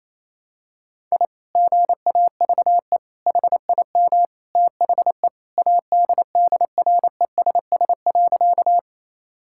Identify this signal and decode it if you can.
Morse code, heard as a keyed tone, about 27 wpm, 700 Hz. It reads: I GAVE HIM THE ADDRESS.
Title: Sentences from Top 1000 Words - Review Part 6 - 28wpm